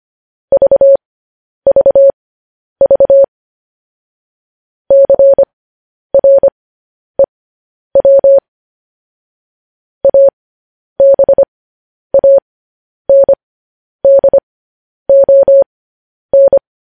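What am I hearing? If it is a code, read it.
Morse code, heard as a keyed tone, about 25 wpm, spaced as for 10 wpm, 550 Hz. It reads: VVV CREW ABANDON
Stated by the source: Morse news headlines practice at 10.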